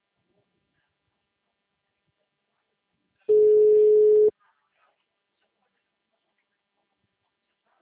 Ringback tone (Europe, 425 Hz).
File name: out- -2007